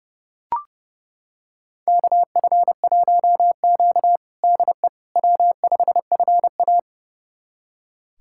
Morse code, heard as a keyed tone, about 30 wpm, 700 Hz. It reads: KF1Q DE W5FA
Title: USA Callsign DE Callsign - 4 Characters - 30wpm